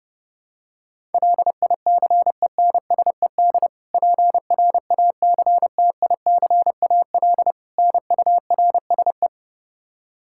Morse code, heard as a keyed tone, about 30 wpm, 700 Hz. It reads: LICENSED PRACTICAL NURSE
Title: Practice QSO - Exchange 3 - More Info - 30wpm